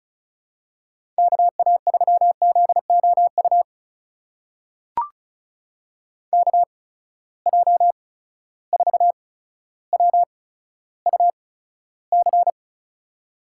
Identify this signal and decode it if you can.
Morse code, heard as a keyed tone, about 36 wpm, spaced as for 15 wpm, 700 Hz. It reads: KA3ZOU KJ4WUC
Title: USA Call Signs - 8x Character Spacing - 35wpm